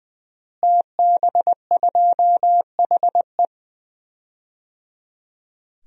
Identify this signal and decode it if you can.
Morse code, heard as a keyed tone, about 21 wpm, 700 Hz. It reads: TB2HE
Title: DXCC-100 Call Signs - 5 Characters - Repeated 3x - 20wpm